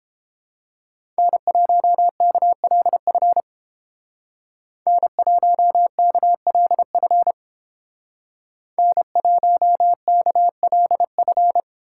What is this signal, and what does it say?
Morse code, heard as a keyed tone, about 31 wpm, 700 Hz. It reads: N1KLF N1KLF N1KLF